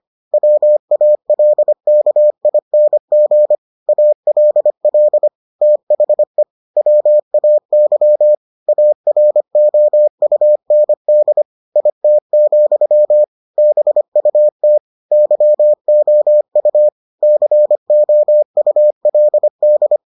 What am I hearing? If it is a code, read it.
Morse code, heard as a keyed tone, about 25 wpm, 600 Hz. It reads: WALKING ALL THE WAY AROUND IT, BUT YOU COULD